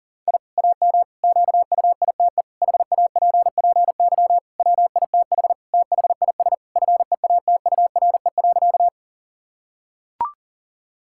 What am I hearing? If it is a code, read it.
Morse code, heard as a keyed tone, about 40 wpm, 700 Hz. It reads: I AM QUITE HAPPY WITH THIS FEATURE.
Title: Sentences from Top 1000 Words - Review Part 5 - 40wpm